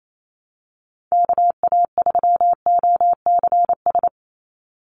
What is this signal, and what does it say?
Morse code, heard as a keyed tone, about 28 wpm, 700 Hz. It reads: KA3OCS
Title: Call Signs - 28wpm